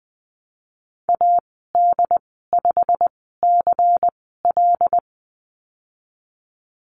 Morse code, heard as a keyed tone, about 20 wpm, spaced as for 14 wpm, 700 Hz. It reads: AD5CL
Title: 128 Lesson 21g - Number 6 - Call Signs 20wpm